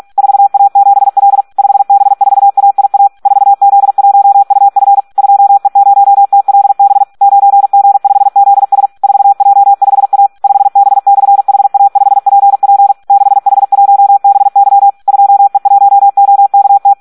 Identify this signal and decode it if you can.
Morse code, heard as a keyed tone, about 46 wpm, 800 Hz. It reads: 4A7L 5B4AIA 4Z1UF 2E0TLB 9G5ZS 4J5A 5B/HA5PP 6H1BY 2E1OKT
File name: Audio practica CW / SD 1 46